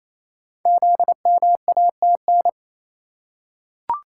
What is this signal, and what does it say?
Morse code, heard as a keyed tone, about 28 wpm, 700 Hz. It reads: ZMATN
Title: Code Groups - 5 Letters - 28wpm